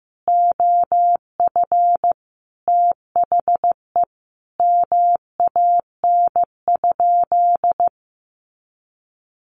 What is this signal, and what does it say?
Morse code, heard as a keyed tone, about 15 wpm, 700 Hz. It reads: OF THE MAN?
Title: Sentences from Top 1300 Words 15wpm